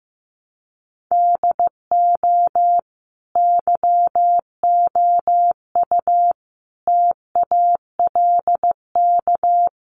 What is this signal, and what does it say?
Morse code, heard as a keyed tone, about 15 wpm, 700 Hz. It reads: DO YOU TALK
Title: Sentences from Top 500 Words 15wpm